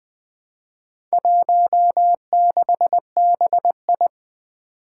Morse code, heard as a keyed tone, about 21 wpm, 700 Hz. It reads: 16BI
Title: Four Challenging Letter-Number Combinations 20wpm